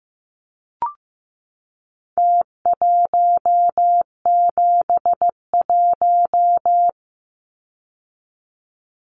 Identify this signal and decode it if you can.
Morse code, heard as a keyed tone, about 15 wpm, 700 Hz. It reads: T171
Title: Numbers - Contest Style - 4-Digit 15wpm